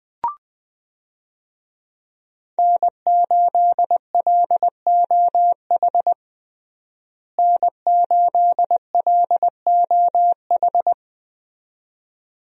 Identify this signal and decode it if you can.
Morse code, heard as a keyed tone, about 20 wpm, 700 Hz. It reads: N8LOH N8LOH
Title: USA Call Signs - 5 Characters - Repeated 2x - 20wpm